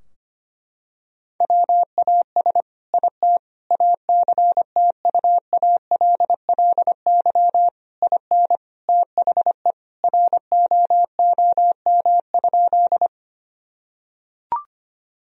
Morse code, heard as a keyed tone, about 25 wpm, 700 Hz. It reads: WAS IT ACTUALLY IN THE ROOM?
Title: Sentences from Top 1000 Words - Review Part 4 - 25wpm